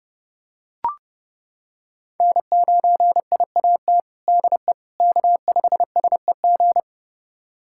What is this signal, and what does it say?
Morse code, heard as a keyed tone, about 32 wpm, 700 Hz. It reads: N9IAT DE K5SEG
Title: USA Callsign DE Callsign - 5 Characters - 30wpm